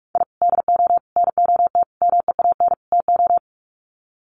Morse code, heard as a keyed tone, about 44 wpm, 700 Hz. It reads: I DO NOT MEAN TO
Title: Sets of 5 Words 45wpm